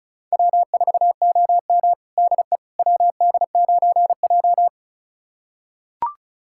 Morse code, heard as a keyed tone, about 36 wpm, 700 Hz. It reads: W4OM DE WD9J
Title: USA Callsign DE Callsign - 4 Characters - 35wpm